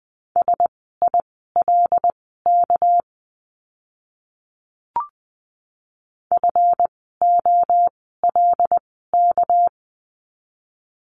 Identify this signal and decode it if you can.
Morse code, heard as a keyed tone, about 20 wpm, spaced as for 14 wpm, 700 Hz. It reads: SILK FOLK
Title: 199 Lesson 32f - Letter k - Words 20wpm